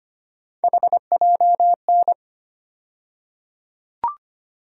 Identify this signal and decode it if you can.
Morse code, heard as a keyed tone, about 25 wpm, 700 Hz. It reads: HJN
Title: USA Call Sign Postfixes - 3 Letters 25wpm